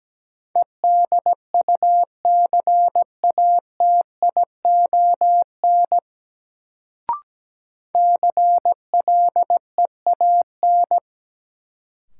Morse code, heard as a keyed tone, about 17 wpm, 700 Hz. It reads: EDUCATION CLEAN
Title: Sentences from Top 1100 Words - New Word Drill Alt - 17wpm